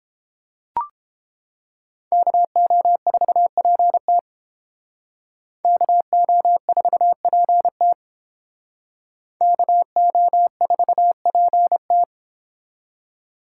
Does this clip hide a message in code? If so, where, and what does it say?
Yes: Morse code from 2.1 to 12.0 s, heard as a keyed tone, about 29 wpm, 700 Hz. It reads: KO4PT KO4PT KO4PT